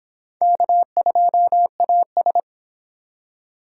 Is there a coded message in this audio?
Yes: Morse code from 0.4 to 2.4 s, heard as a keyed tone, about 22 wpm, 700 Hz. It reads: K2AS